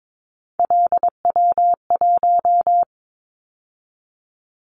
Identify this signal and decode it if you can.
Morse code, heard as a keyed tone, about 22 wpm, 700 Hz. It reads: LW1
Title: Callsign Prefixes for Top 50 DXCC - 22wpm